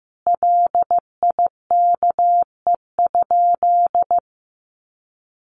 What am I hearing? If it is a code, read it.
Morse code, heard as a keyed tone, about 15 wpm, 700 Hz. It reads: LIKE?
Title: Sentences from Top 1200 Words 15wpm